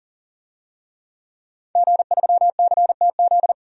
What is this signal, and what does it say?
Morse code, heard as a keyed tone, about 40 wpm, 700 Hz. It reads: G3CTZ
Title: DXCC-100 Call Signs - 5 Characters - 40wpm